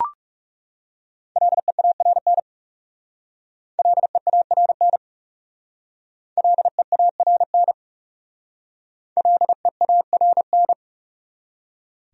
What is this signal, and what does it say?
Morse code, heard as a keyed tone, about 36 wpm, 700 Hz. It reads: LEARN LEARN LEARN LEARN
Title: Speed-Racing Top 500 Words 30wpm